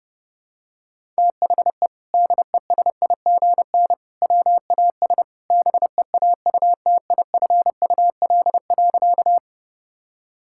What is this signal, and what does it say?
Morse code, heard as a keyed tone, about 32 wpm, 700 Hz. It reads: THE DESIGN WAS BEAUTIFUL.
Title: Sentences from Top 1000 Words - Review Part 5 - 30wpm